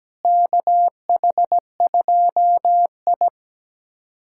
Morse code, heard as a keyed tone, about 17 wpm, 700 Hz. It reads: KH2I
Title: USA Callsign DE Callsign - 4 Characters - 17wpm